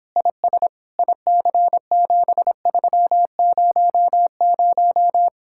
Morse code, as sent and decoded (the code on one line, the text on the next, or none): .. ... / .. -.-. --... ...-- ----- -----
IS IC7300